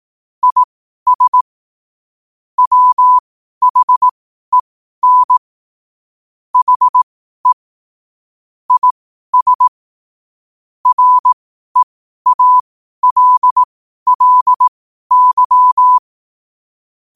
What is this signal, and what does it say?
Morse code, heard as a keyed tone, about 18 wpm, spaced as for 12 wpm, 1000 Hz. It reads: IS WHEN HE IS REALLY
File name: 12wpm